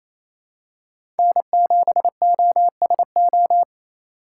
Morse code, heard as a keyed tone, about 28 wpm, 700 Hz. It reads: N7OSO
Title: USA Call Signs - 5 Characters - 2x Character Spacing - 28wpm